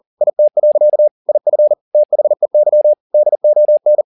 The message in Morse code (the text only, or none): IT. IF THEY DON